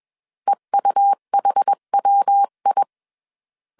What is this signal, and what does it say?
Morse code, heard as a keyed tone, about 21 wpm, 800 Hz. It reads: EUHWI
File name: 20wpm